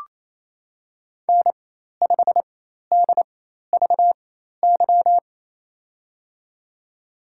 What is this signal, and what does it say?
Morse code, heard as a keyed tone, about 27 wpm, spaced as for 13 wpm, 700 Hz. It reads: N5DVY
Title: USA Call Signs - 4x Character Spacing - 28wpm